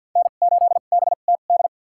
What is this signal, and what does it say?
Morse code, heard as a keyed tone, about 48 wpm, spaced as for 35 wpm, 700 Hz. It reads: N8BTD
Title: USA Call Signs - 5 Characters - 2x Character Spacing - 50wpm